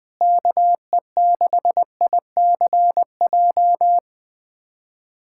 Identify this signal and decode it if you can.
Morse code, heard as a keyed tone, about 21 wpm, 700 Hz. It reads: KE6ICJ
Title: USA Callsign DE Callsign 20wpm